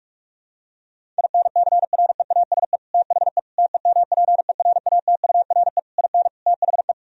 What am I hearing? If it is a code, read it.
Morse code, heard as a keyed tone, about 43 wpm, 700 Hz. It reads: INCREASE THE TEMPERATURE IN THE
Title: Sentences from Top 1000 Words - Review Part 4 - 45wpm